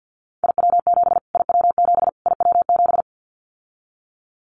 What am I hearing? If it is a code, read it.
Morse code, heard as a keyed tone, about 50 wpm, 700 Hz. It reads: IW7 IW7 IW7
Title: Callsign Prefixes for Top 25 DXCC - Repeated 3x - 50wpm